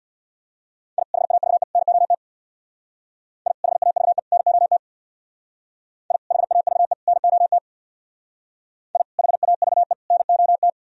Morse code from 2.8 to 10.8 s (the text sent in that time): I HAVE NOT I HAVE NOT I HAVE NOT